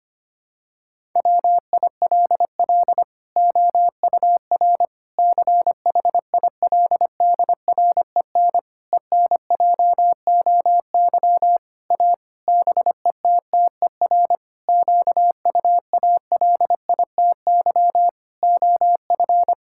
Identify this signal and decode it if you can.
Morse code, heard as a keyed tone, about 25 wpm, 700 Hz. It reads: WILL OUR CHILDREN ENJOY A BETTER QUALITY OF
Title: Sentences from Top 1000 Words - Review Part 5 - 25wpm